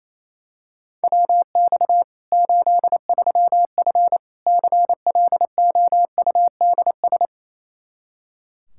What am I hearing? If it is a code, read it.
Morse code, heard as a keyed tone, about 28 wpm, 700 Hz. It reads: WX 83F CLOUDS
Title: Practice QSO - Exchange 2 - Info - 28wpm